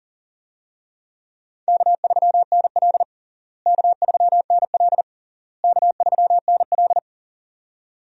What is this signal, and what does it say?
Morse code, heard as a keyed tone, about 40 wpm, 700 Hz. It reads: K3NL K3NL K3NL
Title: USA Call Signs - 4 Characters - Repeated 3x - 40wpm